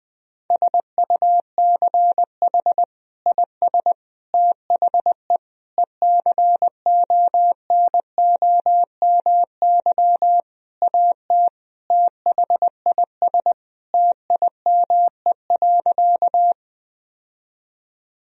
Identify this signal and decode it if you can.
Morse code, heard as a keyed tone, about 20 wpm, 700 Hz. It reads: SUCH IS THE ECONOMY AT THIS TIME.
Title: Sentences from Top 1000 Words - Review Part 3 - 20wpm